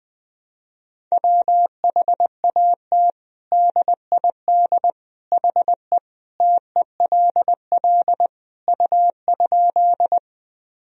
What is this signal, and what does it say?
Morse code, heard as a keyed tone, about 20 wpm, 700 Hz. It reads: WHAT DID HE TELL U?
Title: Sentences from Top 200 Words and Abbreviations 20wpm